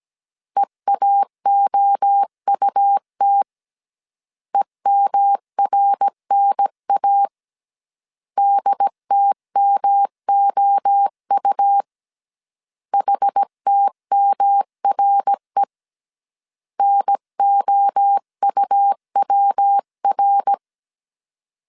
Morse code, heard as a keyed tone, about 17 wpm, 800 Hz. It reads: EAOUT EMRNA DTMOU HTMRE NOUWR
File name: CW / DET / 1 3 1500C S A 16wpm 2141